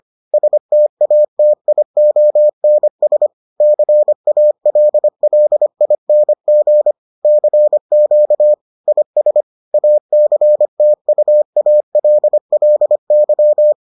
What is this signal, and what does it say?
Morse code, heard as a keyed tone, about 25 wpm, 600 Hz. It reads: STATIONS CALLING CQ IS ACTUALLY